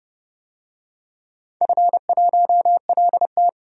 Morse code, heard as a keyed tone, about 30 wpm, 700 Hz. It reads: F1LT
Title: DXCC-100 Call Signs - 4 Characters - 30wpm